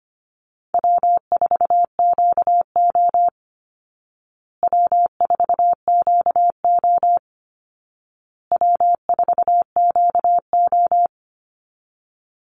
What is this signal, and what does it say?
Morse code, heard as a keyed tone, about 25 wpm, 700 Hz. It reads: W4QO W4QO W4QO
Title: Easy Call Signs Repeated 3x - 25wpm